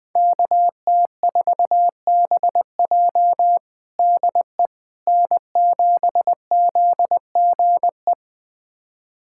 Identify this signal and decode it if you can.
Morse code, heard as a keyed tone, about 20 wpm, 700 Hz. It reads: KT4BJ DE N7ZGE
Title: USA Callsign DE Callsign - 5 Characters - 20wpm